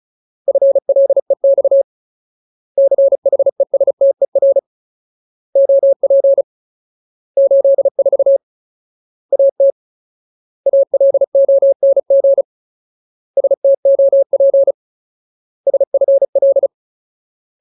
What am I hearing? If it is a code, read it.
Morse code, heard as a keyed tone, about 34 wpm, 550 Hz. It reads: FLEX CHESTER OP 84 AT ALONG STOP SFL
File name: Ultimate Word List / 35WPM Ultimate